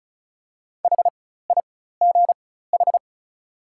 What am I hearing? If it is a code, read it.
Morse code, heard as a keyed tone, about 34 wpm, spaced as for 16 wpm, 690 Hz. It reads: HIGH
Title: 4x Character Spacing - Top 300 Words - 35wpm